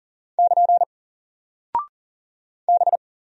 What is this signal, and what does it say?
Morse code, heard as a keyed tone, about 40 wpm, 700 Hz. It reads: <KN> B
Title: Single Character (Letter:Number:Punctuation:Prosign) Alt 40wpm